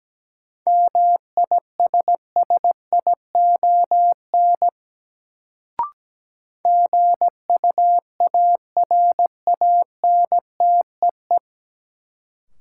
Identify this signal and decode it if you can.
Morse code, heard as a keyed tone, about 17 wpm, 700 Hz. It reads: MISSION GUARANTEE
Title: Sentences from Top 1600 Words - New Word Drill Alt - 17wpm